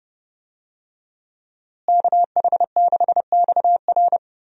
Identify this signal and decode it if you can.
Morse code, heard as a keyed tone, about 32 wpm, 700 Hz. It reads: KH6XR
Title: DXCC-100 Call Signs - 5 Characters - Repeated 3x - 30wpm